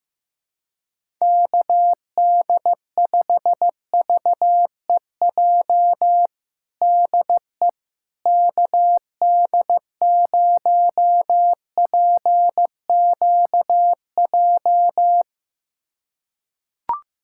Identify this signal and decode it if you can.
Morse code, heard as a keyed tone, about 15 wpm, 700 Hz. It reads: KD5VEJ DE KD0PQJ
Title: USA Callsign DE Callsign - 6 Characters - 15wpm